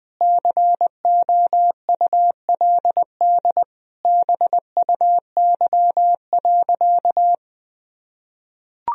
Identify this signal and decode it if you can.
Morse code, heard as a keyed tone, about 20 wpm, 700 Hz. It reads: COULD BUY.
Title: Sentences from Top 1000 Words - Review Part 4 - 20wpm